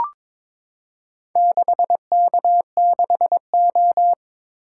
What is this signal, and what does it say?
Morse code, heard as a keyed tone, about 22 wpm, 700 Hz. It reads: 6K6O
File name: practice / 22wpm / Four Letter-Number Combinations - 22wpm